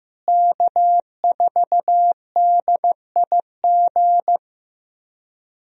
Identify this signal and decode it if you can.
Morse code, heard as a keyed tone, about 15 wpm, 700 Hz. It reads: K4DIG
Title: USA Call Signs - 5 Characters - Repeated 2x - 15wpm